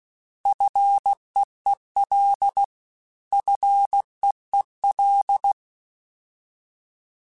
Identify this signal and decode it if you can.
Morse code, heard as a keyed tone, about 16 wpm, 800 Hz. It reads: FEEL FEEL
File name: Short Runs / Short Runs-mp3s / SinglePrefixes @15